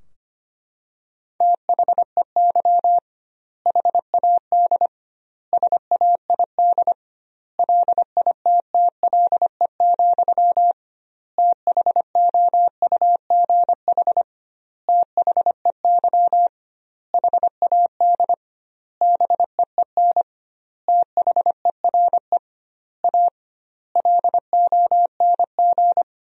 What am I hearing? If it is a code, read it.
Morse code, heard as a keyed tone, about 25 wpm, 700 Hz. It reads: THEY HAD SAID LITTLE, THOUGH THEY HAD BEEN THERE A LONG
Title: 2x Word-Spacing - Sentences from Top 400 Words - 25wpm